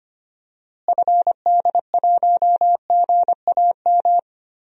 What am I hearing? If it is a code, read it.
Morse code, heard as a keyed tone, about 25 wpm, 700 Hz. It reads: FD1GAM
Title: DXCC-100 Call Signs - 6 Characters - Repeated 2x - 25wpm